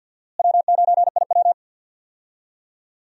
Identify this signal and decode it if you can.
Morse code, heard as a keyed tone, about 52 wpm, 700 Hz. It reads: W9IW
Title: USA Call Signs - 4 Characters - Repeated 2x - 50wpm